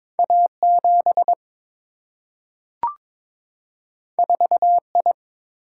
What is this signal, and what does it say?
Morse code, heard as a keyed tone, about 22 wpm, 700 Hz. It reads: A7 4I
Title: Single-Letter Single-Number 22wpm